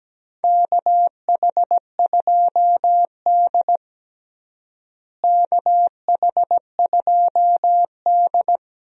Morse code, heard as a keyed tone, about 17 wpm, 700 Hz. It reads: KH2D KH2D
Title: USA Call Signs - 4 Characters - Repeated 2x - 17wpm